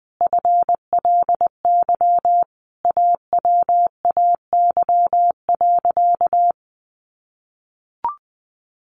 Morse code, heard as a keyed tone, about 20 wpm, 700 Hz. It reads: FLY AWAY.
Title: Sentences from Top 800 Words 20wpm